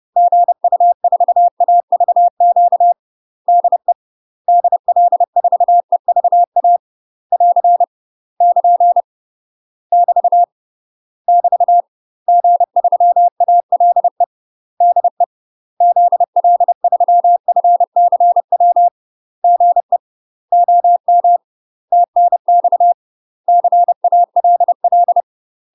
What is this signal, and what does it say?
Morse code, heard as a keyed tone, about 30 wpm, 700 Hz. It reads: GU4AVQ DE DL4EVA <AR> <KN> <BT> <BT> G3ALE DE ZL3FCW GE OM TNX CALL
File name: UK MASTER QSOs TRAINING / 30WPM / UK MASTER QSOs 0020